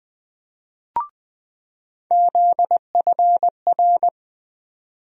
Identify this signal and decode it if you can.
Morse code, heard as a keyed tone, about 20 wpm, 700 Hz. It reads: ZFR
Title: USA Call Sign Postfixes - 3 Letters 20wpm